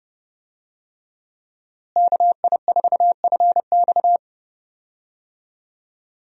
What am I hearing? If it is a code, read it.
Morse code, heard as a keyed tone, about 30 wpm, 700 Hz. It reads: KI4FX
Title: USA Call Signs - 5 Characters - Repeated 3x - 30wpm